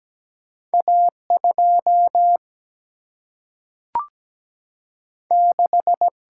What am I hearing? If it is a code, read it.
Morse code, heard as a keyed tone, about 17 wpm, 700 Hz. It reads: A 2 6
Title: Single-Letter Single-Number 17wpm